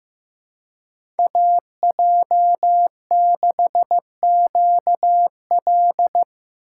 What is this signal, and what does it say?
Morse code, heard as a keyed tone, about 15 wpm, 700 Hz. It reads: AJ6QL
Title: K1USN SST QSO Practice - 15wpm